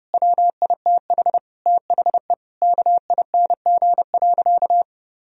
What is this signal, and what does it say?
Morse code, heard as a keyed tone, about 30 wpm, 700 Hz. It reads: WITH THE KING.
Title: Sentences from Top 1000 Words - Review Part 4 - 30wpm